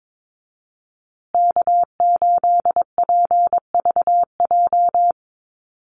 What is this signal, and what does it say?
Morse code, heard as a keyed tone, about 22 wpm, 700 Hz. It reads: K8PVJ
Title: USA Call Signs Repeated 3x 22wpm